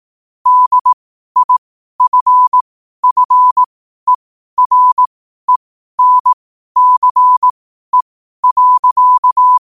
Morse code, heard as a keyed tone, about 18 wpm, spaced as for 12 wpm, 1000 Hz. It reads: DIFFERENCE.
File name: qotdfiles / 12wpm